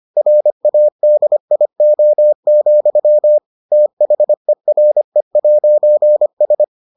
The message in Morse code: RADIO, THERE'S

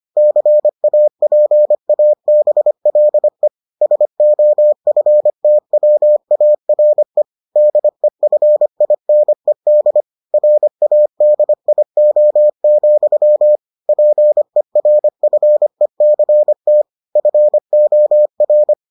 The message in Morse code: CAPABLE SOFTWARE DEFINED RADIO, PERFECT FOR

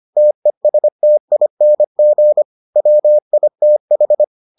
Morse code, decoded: TESTING WITH

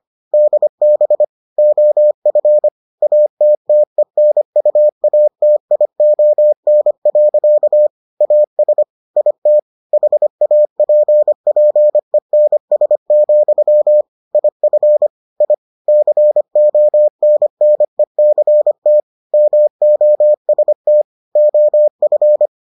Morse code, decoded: DB OF ATTENUATION. AS IT HAPPENS, IF I CONNECT MOST OF

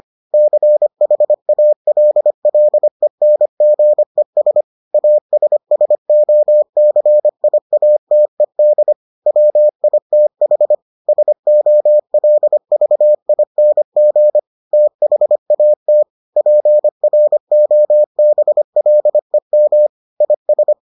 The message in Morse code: CHALLENGES ASSOCIATED WITH SOLVING THAT PROBLEM IS

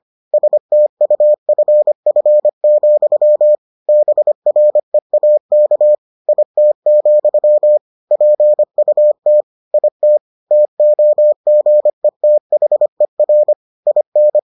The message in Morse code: STUFF, BREAK IT, PUT IT TOGETHER IN

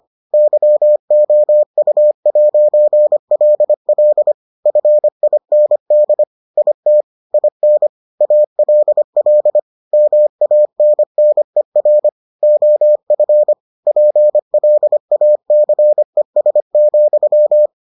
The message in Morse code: YOU'LL FIND IT IN ALL MANNER OF PLACES,

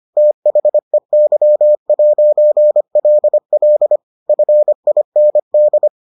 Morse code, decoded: THEY'LL FIND

